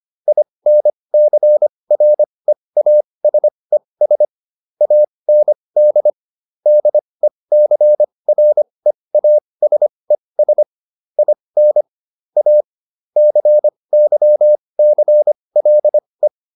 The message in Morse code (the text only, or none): INCREASES AND DECREASES IN A CYCLE